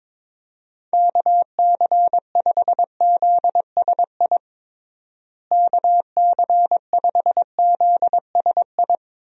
Morse code, heard as a keyed tone, about 22 wpm, 700 Hz. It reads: KC5ZSI KC5ZSI
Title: USA Call Signs - 6 Characters - Repeated 2x - 22wpm